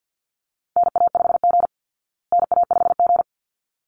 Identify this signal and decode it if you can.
Morse code, heard as a keyed tone, about 49 wpm, 700 Hz. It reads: NA5G NA5G
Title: Call Signs Repeated 3x - 50wpm